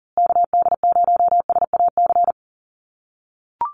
Morse code, heard as a keyed tone, about 40 wpm, 700 Hz. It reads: KD0SAC
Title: USA Call Signs Repeated 2x 40wpm